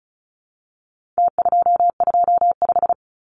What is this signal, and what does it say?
Morse code, heard as a keyed tone, about 34 wpm, 700 Hz. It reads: T225
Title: Numbers - Contest Style - 4-Digit 35wpm